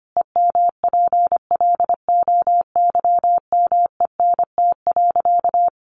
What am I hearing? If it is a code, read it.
Morse code, heard as a keyed tone, about 25 wpm, 700 Hz. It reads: EMPLOYMENT.
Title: Sentences from Top 1500 Words 25wpm